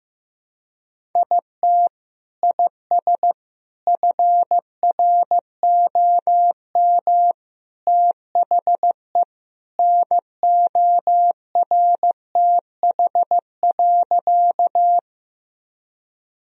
Morse code, heard as a keyed tone, about 15 wpm, 700 Hz. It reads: IT IS FROM THE NORTH.
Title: Sentences from Top 1000 Words - Review Part 1 - 15wpm